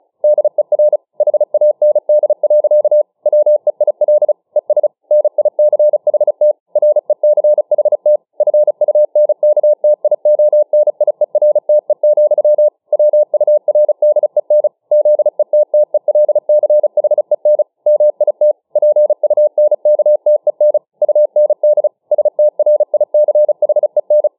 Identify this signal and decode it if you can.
Morse code, heard as a keyed tone, about 34 wpm, 600 Hz. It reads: DER HAND. WEIL ES NICHT RECHT FUNKTIONIERTE, WURDEN ZETTELCHEN MIT PUNKTEN UND STRICHEN